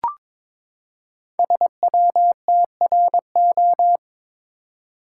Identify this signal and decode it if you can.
Morse code, heard as a keyed tone, about 23 wpm, 700 Hz. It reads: SWTRO
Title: Code Groups - 5 Letters - 22wpm